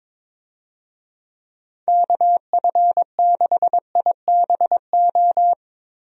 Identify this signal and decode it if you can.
Morse code, heard as a keyed tone, about 22 wpm, 700 Hz. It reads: KF6IBO
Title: USA Call Signs - 6 Characters - Repeated 2x - 22wpm